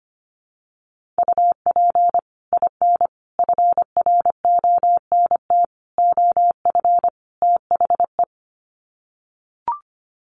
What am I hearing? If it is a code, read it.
Morse code, heard as a keyed tone, about 25 wpm, 700 Hz. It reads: UP IN FRONT OF THE
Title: Sets of 5 Words Encore2 25wpm